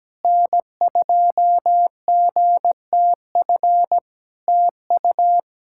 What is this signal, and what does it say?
Morse code, heard as a keyed tone, about 17 wpm, 700 Hz. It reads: N2GTF TU
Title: Practice QSO - Exchange 4 - Ending - 17wpm